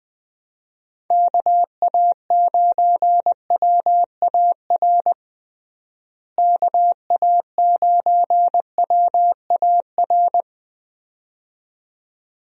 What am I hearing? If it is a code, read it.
Morse code, heard as a keyed tone, about 21 wpm, 700 Hz. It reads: KA9WAR KA9WAR
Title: USA Call Signs - 6 Characters - Repeated 2x - 20wpm